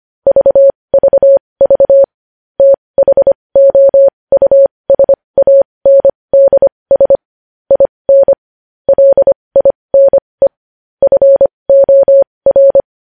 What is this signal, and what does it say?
Morse code, heard as a keyed tone, about 25 wpm, spaced as for 20 wpm, 550 Hz. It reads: VVV THOUSANDS IN LINE FOR
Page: News Headlines in Morse Code at 20 WPM